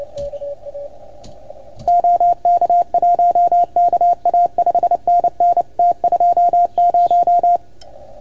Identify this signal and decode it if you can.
Morse code, heard as a keyed tone, about 29 wpm, 690 Hz. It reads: OK1KA 5NN T20
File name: Archiv / CUC310 / OK1FGD CUC 310 pileup